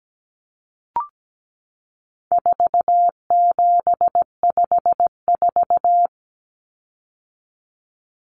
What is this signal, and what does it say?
Morse code, heard as a keyed tone, about 17 wpm, 700 Hz. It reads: 4754
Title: Common Cut Numbers - 4-Digit 17wpm